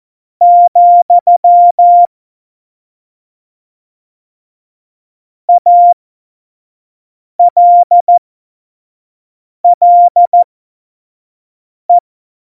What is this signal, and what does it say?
Morse code, heard as a keyed tone, about 14 wpm, spaced as for 5 wpm, 700 Hz. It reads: , ALLE
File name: Morsekurs / Witze / Witz 27 March 2018, 25 F70